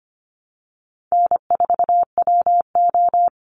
Morse code, heard as a keyed tone, about 25 wpm, 700 Hz. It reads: N4WO
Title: Easy Call Signs - 25wpm